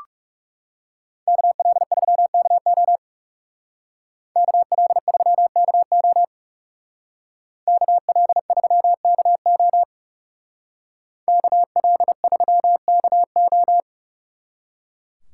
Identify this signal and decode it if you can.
Morse code, heard as a keyed tone, about 39 wpm, 700 Hz. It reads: KL3KO KL3KO KL3KO KL3KO